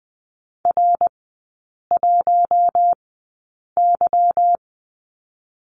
Morse code, heard as a keyed tone, about 19 wpm, 690 Hz. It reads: R 1 Y
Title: 145 Lesson 24e - Letter y - Three Characters 20wpm